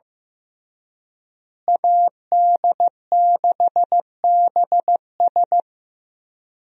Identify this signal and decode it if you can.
Morse code, heard as a keyed tone, about 15 wpm, 700 Hz. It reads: AD6BS
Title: USA Call Signs - 5 Characters - Repeated 3x - 15wpm